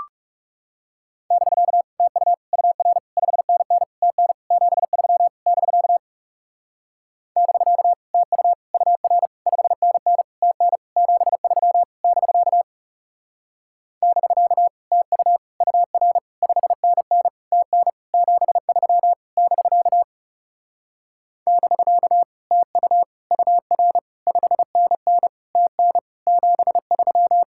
Morse code, heard as a keyed tone, about 37 wpm, 700 Hz. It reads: <BK> TU UR 5NN TN 73 <BK> <BK> TU UR 5NN TN 73 <BK> <BK> TU UR 5NN TN 73 <BK> <BK> TU UR 5NN TN 73
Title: Speed-Racing Common QSO Phrases - 30wpm